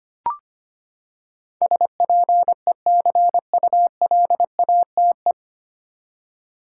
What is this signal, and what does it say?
Morse code, heard as a keyed tone, about 25 wpm, 700 Hz. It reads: SPECULATE
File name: Nine Letter Words - 25wpm